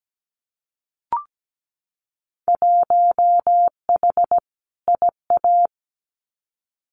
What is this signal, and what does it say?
Morse code, heard as a keyed tone, about 17 wpm, 700 Hz. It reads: 1H IA
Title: Winter Field Day Exchanges 17wpm